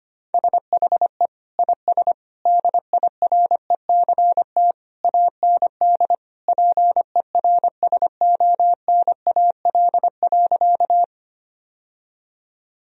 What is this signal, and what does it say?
Morse code, heard as a keyed tone, about 25 wpm, 700 Hz. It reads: SHE IS DIRECT AND PERSONAL.
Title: Sentences from Top 1000 Words - Review Part 4 - 25wpm